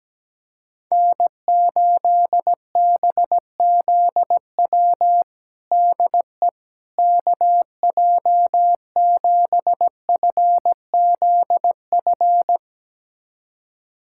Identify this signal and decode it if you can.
Morse code, heard as a keyed tone, about 17 wpm, 700 Hz. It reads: N8BZW DE KJ7FZF
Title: USA Callsign DE Callsign 17wpm